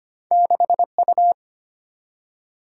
Morse code, heard as a keyed tone, about 25 wpm, 700 Hz. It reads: 6U
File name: Two Letter-Number Combinations - 25wpm